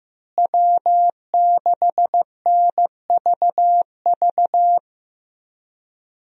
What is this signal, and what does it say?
Morse code, heard as a keyed tone, about 15 wpm, 700 Hz. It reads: W6NVV
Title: USA Call Signs - 5 Characters - Repeated 3x - 15wpm